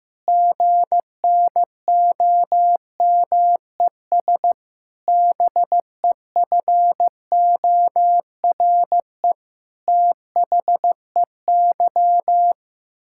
Morse code, heard as a keyed tone, about 15 wpm, 700 Hz. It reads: GNOMES BEFORE THEY
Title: Pangram Challenge v2 - 15wpm